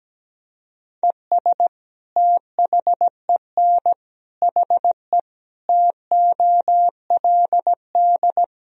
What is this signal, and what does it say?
Morse code, heard as a keyed tone, about 17 wpm, 700 Hz. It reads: ES THEN HE TOLD
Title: Sentences from Top 400 Words and Abbreviations 17wpm